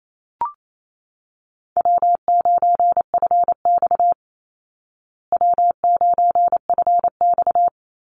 Morse code, heard as a keyed tone, about 28 wpm, 700 Hz. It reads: W9FX W9FX
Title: Easy Call Signs Repeated 2x - 28wpm